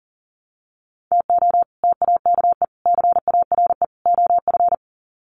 Morse code, heard as a keyed tone, about 40 wpm, 700 Hz. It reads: TO TAKE CARE OF
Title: Sets of 4 Words 40wpm